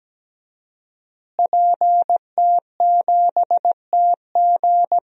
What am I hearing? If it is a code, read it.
Morse code, heard as a keyed tone, about 17 wpm, 700 Hz. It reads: PT7TG
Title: DXCC-100 Call Signs - 5 Characters - Repeated 2x - 17wpm